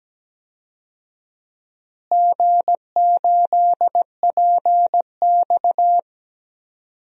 Morse code, heard as a keyed tone, about 17 wpm, 700 Hz. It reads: G8PX
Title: DXCC-100 Call Signs - 4 Characters - Repeated 3x - 17wpm